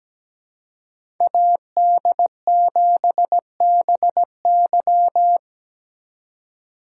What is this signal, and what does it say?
Morse code, heard as a keyed tone, about 17 wpm, 700 Hz. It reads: AD7BY
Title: USA Call Signs - 4x Character Spacing - 17wpm